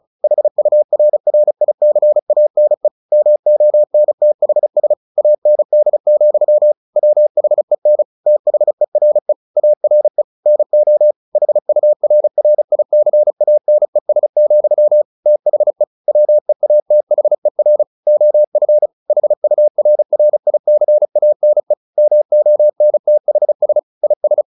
Morse code, heard as a keyed tone, about 34 wpm, 600 Hz. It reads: HURRICANE MONTHS AND, WHEN THERE ARE NO HURRICANES, THE WEATHER OF HURRICANE MONTHS IS
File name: the old man and the sea / the old man and the sea 1 35wpm